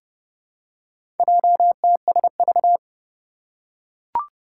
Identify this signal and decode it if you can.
Morse code, heard as a keyed tone, about 29 wpm, 700 Hz. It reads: JTSV
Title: Four Challenging Letter-Number Combinations 30wpm